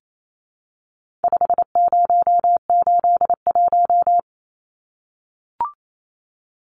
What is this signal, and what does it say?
Morse code, heard as a keyed tone, about 28 wpm, 700 Hz. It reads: 5081
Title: Numbers - 4-Digit 28wpm